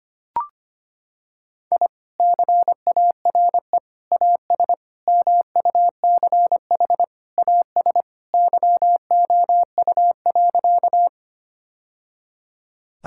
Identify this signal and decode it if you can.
Morse code, heard as a keyed tone, about 25 wpm, 700 Hz. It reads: I CARE AS MUCH AS YOU.
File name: practice / 25wpm / Sentences with 3-Word Phrases - 25wpm